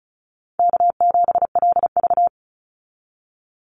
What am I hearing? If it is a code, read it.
Morse code, heard as a keyed tone, about 34 wpm, 700 Hz. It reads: K7LV
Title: Easy Call Signs Repeated 2x - 35wpm